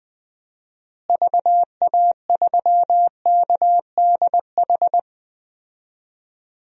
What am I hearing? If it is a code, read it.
Morse code, heard as a keyed tone, about 21 wpm, 700 Hz. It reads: VA3KDH
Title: DXCC-100 Call Signs - 6 Characters - Repeated 2x - 20wpm